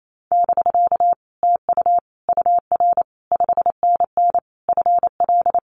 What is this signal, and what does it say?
Morse code, heard as a keyed tone, about 28 wpm, 700 Hz. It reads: <BK> TU UR 5NN FL
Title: Common QSO Phrases Encore 28wpm